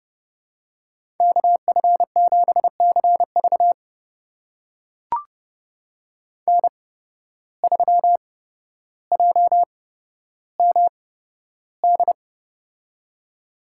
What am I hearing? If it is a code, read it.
Morse code, heard as a keyed tone, about 30 wpm, spaced as for 13 wpm, 700 Hz. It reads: KF7CV N3JMD
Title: USA Call Signs - 5 Characters - 8x Character Spacing - 30wpm